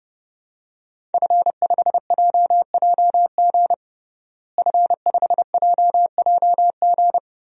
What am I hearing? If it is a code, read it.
Morse code, heard as a keyed tone, about 32 wpm, 700 Hz. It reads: F5JJG F5JJG
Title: DXCC-100 Call Signs - 5 Characters - Repeated 2x - 30wpm